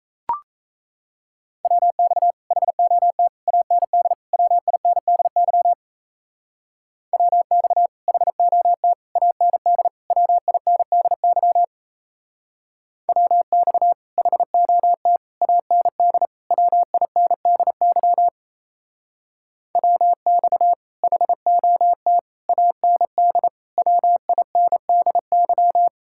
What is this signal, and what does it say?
Morse code, heard as a keyed tone, about 33 wpm, 700 Hz. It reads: WX HOT AND WINDY WX HOT AND WINDY WX HOT AND WINDY WX HOT AND WINDY
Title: Speed-Racing Common QSO Phrases - 28wpm